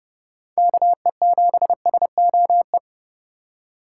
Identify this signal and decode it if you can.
Morse code, heard as a keyed tone, about 30 wpm, 700 Hz. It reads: KE7SOE
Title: USA Call Signs - 6 Characters - 2x Character Spacing - 30wpm